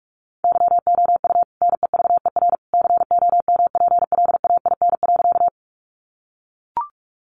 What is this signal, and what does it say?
Morse code, heard as a keyed tone, about 44 wpm, 700 Hz. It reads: YOU NEVER COMPLAIN.
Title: Sentences from Top 1300 Words 45wpm